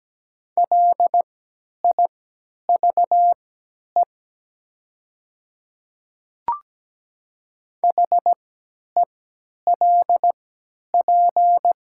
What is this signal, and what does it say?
Morse code, heard as a keyed tone, about 17 wpm, spaced as for 10 wpm, 690 Hz. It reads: LIVE HELP
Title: 3x Character Spacing - Top 200 Words - 17wpm